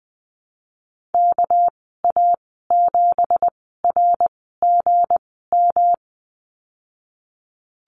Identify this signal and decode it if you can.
Morse code, heard as a keyed tone, about 20 wpm, spaced as for 14 wpm, 700 Hz. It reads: KA7RGM
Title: 213 Lesson 34g - Number 8 - Call Signs 20wpm